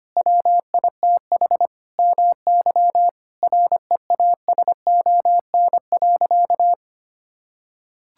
Morse code, heard as a keyed tone, about 25 wpm, 700 Hz. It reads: WITH MY REASON.
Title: Sentences from Top 1000 Words - Review Part 5 - 25wpm